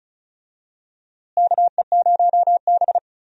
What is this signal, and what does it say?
Morse code, heard as a keyed tone, about 36 wpm, 700 Hz. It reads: KE0B
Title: USA Call Signs - 4 Characters - Repeated 3x - 35wpm